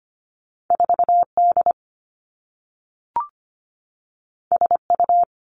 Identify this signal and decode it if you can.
Morse code, heard as a keyed tone, about 25 wpm, 700 Hz. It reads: 4 D S U
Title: Two Challenging Letter-Number Combinations 25wpm